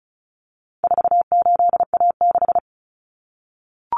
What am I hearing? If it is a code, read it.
Morse code, heard as a keyed tone, about 34 wpm, 700 Hz. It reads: 48A6
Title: Common Cut Numbers - 4-Digit 35wpm